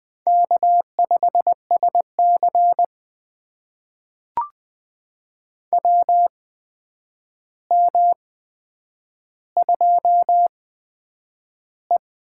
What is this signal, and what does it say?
Morse code, heard as a keyed tone, about 20 wpm, spaced as for 9 wpm, 700 Hz. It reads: K5SC WM2E
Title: USA Call Signs - 4 Characters - 8x Character Spacing - 20wpm